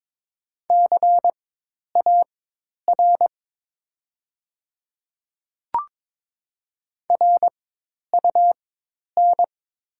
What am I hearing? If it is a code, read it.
Morse code, heard as a keyed tone, about 22 wpm, spaced as for 10 wpm, 700 Hz. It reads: CAR RUN
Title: 4x Character Spacing - Top 300 Words - 22wpm